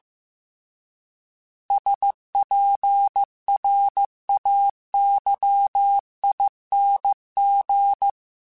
Morse code, heard as a keyed tone, about 15 wpm, 800 Hz. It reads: SPRAYING